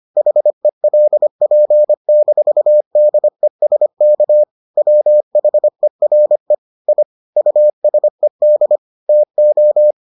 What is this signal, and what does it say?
Morse code, heard as a keyed tone, about 25 wpm, 600 Hz. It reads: HELP-DESK WHERE I USED TO